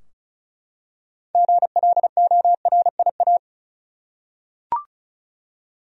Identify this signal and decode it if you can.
Morse code, heard as a keyed tone, about 36 wpm, 700 Hz. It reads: GLORIA
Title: USA Common Names - 35wpm